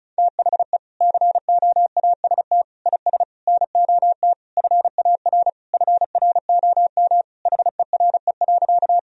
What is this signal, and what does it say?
Morse code, heard as a keyed tone, about 34 wpm, 700 Hz. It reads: THE COAST IS NOT FAR FROM HERE.
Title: Sentences from Top 1000 Words - Review Part 4 - 35wpm